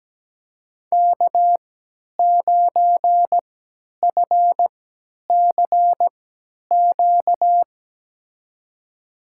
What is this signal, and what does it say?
Morse code, heard as a keyed tone, about 17 wpm, spaced as for 10 wpm, 700 Hz. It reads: K9FCQ
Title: USA Call Signs - 3x Character Spacing - 17wpm